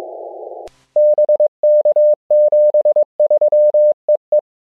Morse code, heard as a keyed tone, about 22 wpm, 600 Hz. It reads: BK 73 EE
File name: ja6pbo-g8bcg